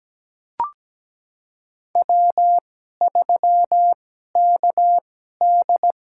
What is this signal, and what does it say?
Morse code, heard as a keyed tone, about 17 wpm, spaced as for 12 wpm, 700 Hz. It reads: W3KD
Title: USA Call Signs - 4 Characters - 2x Character Spacing - 17wpm